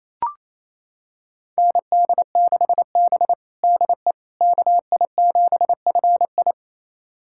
Morse code, heard as a keyed tone, about 28 wpm, 700 Hz. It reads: ND6B DE KI7FI
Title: USA Callsign DE Callsign 28wpm